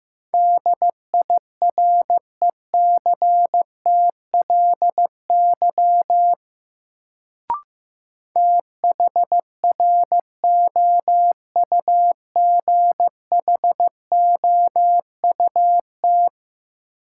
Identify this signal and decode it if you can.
Morse code, heard as a keyed tone, about 15 wpm, 700 Hz. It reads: DIRECTLY THROUGHOUT
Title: Sentences from Top 800 Words - New Word Drill - 15wpm